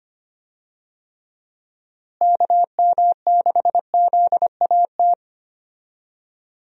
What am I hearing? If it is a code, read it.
Morse code, heard as a keyed tone, about 25 wpm, 700 Hz. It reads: KM6ZAT
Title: USA Call Signs - 6 Characters - Repeated 2x - 25wpm